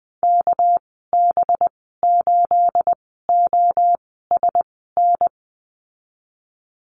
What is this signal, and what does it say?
Morse code, heard as a keyed tone, about 20 wpm, spaced as for 15 wpm, 700 Hz. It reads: KB8OSN
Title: 219 Lesson 35g - Number 0 - Call Signs 20wpm